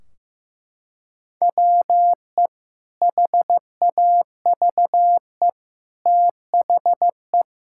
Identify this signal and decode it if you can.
Morse code, heard as a keyed tone, about 15 wpm, 700 Hz. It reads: WE HAVE THE
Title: Sentences from Top 1600 Words 15wpm